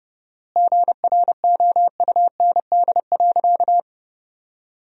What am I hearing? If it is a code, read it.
Morse code, heard as a keyed tone, about 29 wpm, 700 Hz. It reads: GROUND.
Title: 4x Word-Spacing - Sentences from Top 500 Words - 30wpm